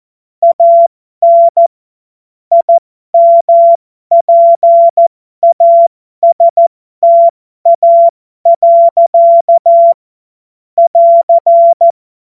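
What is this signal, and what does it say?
Morse code, heard as a keyed tone, about 14 wpm, spaced as for 12 wpm, 680 Hz. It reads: AN IMPASTA. <AR>
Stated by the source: Amateur radio CW practice text made of jokes.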